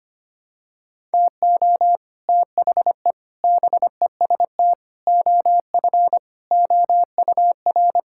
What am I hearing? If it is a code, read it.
Morse code, heard as a keyed tone, about 25 wpm, 700 Hz. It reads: TO THE BEST OF OUR
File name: practice / 25wpm / Sentences with 9-Word Phrases - 25wpm